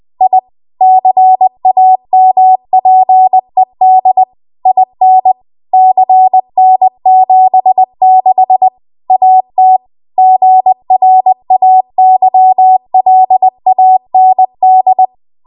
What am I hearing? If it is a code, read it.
Morse code, heard as a keyed tone, about 20 wpm, 750 Hz. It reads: I CAMPED IN CN76 AT GRAYLAND
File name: Morse / 20WPM